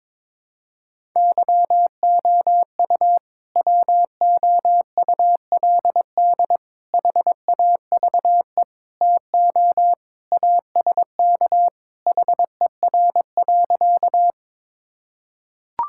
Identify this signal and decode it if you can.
Morse code, heard as a keyed tone, about 22 wpm, 700 Hz. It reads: YOU WOULD HAVE TO ASK HER.
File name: practice / 22wpm / Sentences with 3-Word Phrases - 22wpm